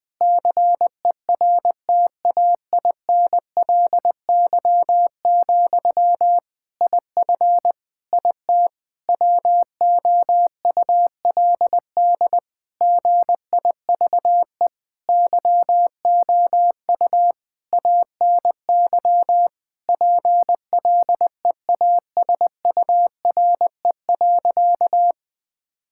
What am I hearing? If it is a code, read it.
Morse code, heard as a keyed tone, about 20 wpm, 700 Hz. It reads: CERTAINLY, IF IT WOULD GIVE YOU ANY PLEASURE.
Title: Sentences from Top 1600 Words 20wpm